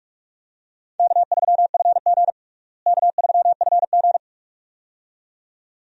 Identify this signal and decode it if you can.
Morse code, heard as a keyed tone, about 42 wpm, 700 Hz. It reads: K3FG K3FG
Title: USA Call Signs - 4 Characters - Repeated 2x - 45wpm